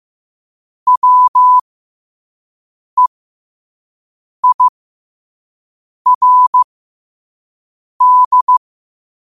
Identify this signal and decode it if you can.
Morse code, heard as a keyed tone, about 15 wpm, spaced as for 5 wpm, 1000 Hz. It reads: WEIRD
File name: qotdfiles / 5wpm